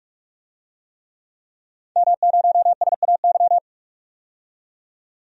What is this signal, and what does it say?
Morse code, heard as a keyed tone, about 46 wpm, 700 Hz. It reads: M0SAY
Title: DXCC-100 Call Signs - 5 Characters - 45wpm